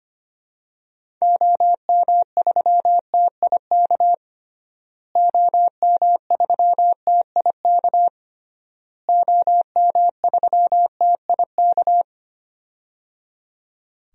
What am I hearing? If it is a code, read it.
Morse code, heard as a keyed tone, about 25 wpm, 700 Hz. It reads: OM3TIK OM3TIK OM3TIK
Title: DXCC-100 Call Signs - 6 Characters - Repeated 3x - 25wpm